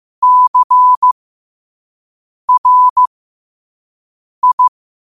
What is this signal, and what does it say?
Morse code, heard as a keyed tone, about 15 wpm, spaced as for 5 wpm, 1000 Hz. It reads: CRI
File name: 5wpm